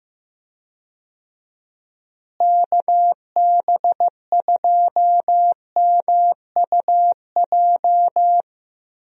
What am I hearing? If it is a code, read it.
Morse code, heard as a keyed tone, about 15 wpm, 700 Hz. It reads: KB2MUJ
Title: USA Call Signs - 6 Characters - Repeated 3x - 15wpm